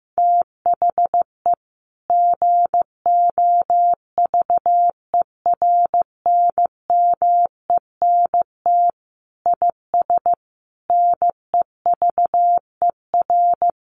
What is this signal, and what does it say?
Morse code, heard as a keyed tone, about 15 wpm, 700 Hz. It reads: THE GOVERNMENT IS NEVER
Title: Sentences from Top 1300 Words 15wpm